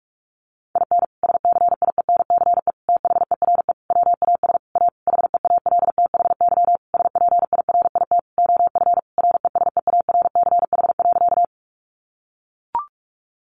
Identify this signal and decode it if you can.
Morse code, heard as a keyed tone, about 44 wpm, 700 Hz. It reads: IN SCIENCE THERE WAS A HEALTHY SPIRIT OF RESEARCH.
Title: Sentences from Top 1400 Words 45wpm